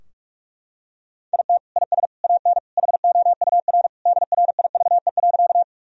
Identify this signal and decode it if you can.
Morse code, heard as a keyed tone, about 41 wpm, 700 Hz. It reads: IT IS AN HOUR DRIVE.
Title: Sentences from Top 1000 Words - Review Part 2 - 45wpm